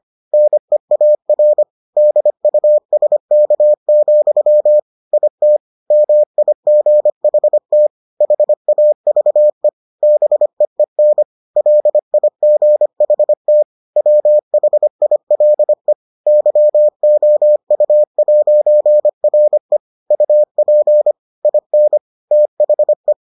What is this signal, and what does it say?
Morse code, heard as a keyed tone, about 25 wpm, 600 Hz. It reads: NEAR DUSK, IT MIGHT HAVE BEEN LIGHT WHILE YOU'RE UP IN THE